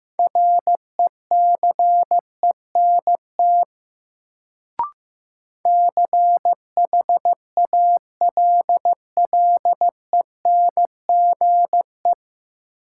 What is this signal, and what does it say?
Morse code, heard as a keyed tone, about 15 wpm, 700 Hz. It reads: RECENT CHALLENGE
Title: Sentences from Top 1000 Words - Review Part 7 - New Word Drill - 15wpm